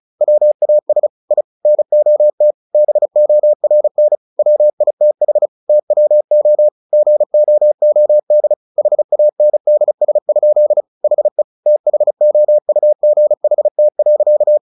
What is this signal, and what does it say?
Morse code, heard as a keyed tone, about 34 wpm, 600 Hz. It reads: WAS I NOT BORN WITH TWO GOOD HANDS? HE THOUGHT.
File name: the old man and the sea 2 35wpm